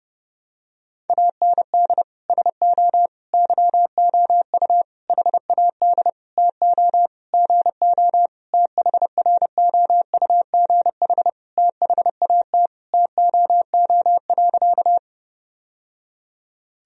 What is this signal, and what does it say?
Morse code, heard as a keyed tone, about 32 wpm, 700 Hz. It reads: AND SO YOU HAD TO GO THROUGH THAT TOO.
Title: Sentences from Top 1000 Words - Review Part 1 - 30wpm